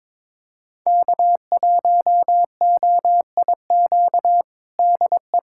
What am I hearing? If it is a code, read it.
Morse code, heard as a keyed tone, about 22 wpm, 700 Hz. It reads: K1OIQ DE
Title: USA Callsign DE Callsign - 5 Characters - 22wpm